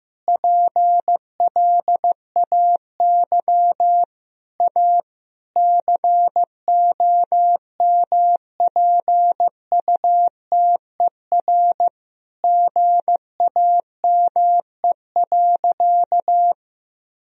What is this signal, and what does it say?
Morse code, heard as a keyed tone, about 15 wpm, 700 Hz. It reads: PLAY A COMPUTER GAME.
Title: Sentences from Top 1000 Words - Review Part 3 - 15wpm